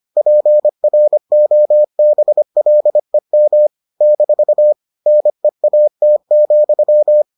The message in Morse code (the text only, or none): PROBLEM - NEAT,